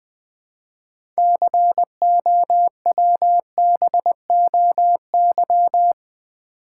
Morse code, heard as a keyed tone, about 20 wpm, 700 Hz. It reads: COWBOY
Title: 6 Letter Words - First 3 Letters Given - 20wpm